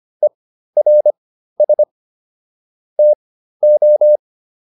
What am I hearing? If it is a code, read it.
Morse code, heard as a keyed tone, about 25 wpm, spaced as for 13 wpm, 600 Hz. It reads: ERS TO